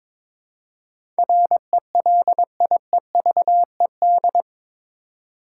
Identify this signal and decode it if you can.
Morse code, heard as a keyed tone, about 21 wpm, 700 Hz. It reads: RELIEVED
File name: practice / 22wpm / Eight Letter Words - 22wpm